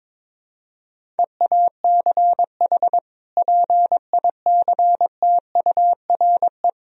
Morse code, heard as a keyed tone, about 21 wpm, 700 Hz. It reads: EACH PICTURE
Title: Sentences from Top 1000 Words - Review Part 1 - 22wpm